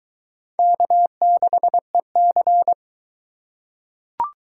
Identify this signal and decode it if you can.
Morse code, heard as a keyed tone, about 21 wpm, 700 Hz. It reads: K6EC